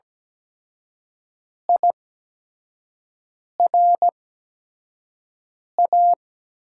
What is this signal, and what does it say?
Morse code, heard as a keyed tone, about 17 wpm, spaced as for 5 wpm, 700 Hz. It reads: IRA
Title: USA Call Signs - 5 Characters - 8x Character Spacing - 17wpm